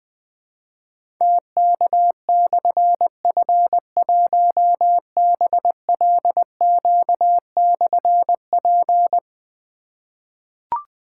Morse code, heard as a keyed tone, about 20 wpm, 700 Hz. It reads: TK/F1BLQ/P
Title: Difficult Call Signs 20wpm